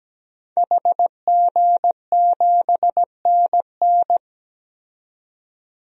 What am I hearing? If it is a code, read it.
Morse code, heard as a keyed tone, about 17 wpm, 700 Hz. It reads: HG7NN
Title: DXCC-100 Call Signs - 5 Characters - 17wpm